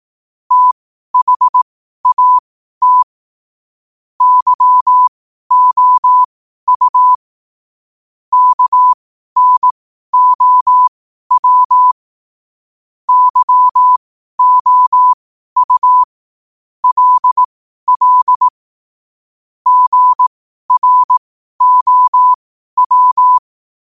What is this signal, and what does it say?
Morse code, heard as a keyed tone, about 18 wpm, spaced as for 12 wpm, 1000 Hz. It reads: THAT YOU KNOW YOU LL GROW
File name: qotdfiles / 12wpm